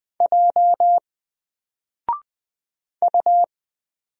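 Morse code, heard as a keyed tone, about 20 wpm, 700 Hz. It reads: J U
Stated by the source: Single letter and number Morse drill at 20 wpm, from N4LCD.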